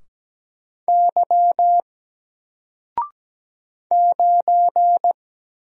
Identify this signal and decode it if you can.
Morse code, heard as a keyed tone, about 17 wpm, 700 Hz. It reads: Y 9
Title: Single Character (Letter:Number:Punctuation:Prosign) 17wpm